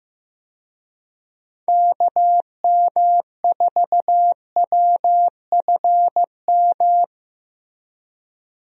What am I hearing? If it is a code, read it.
Morse code, heard as a keyed tone, about 15 wpm, 700 Hz. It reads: KM4WFM
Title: USA Call Signs - 6 Characters - Repeated 3x - 15wpm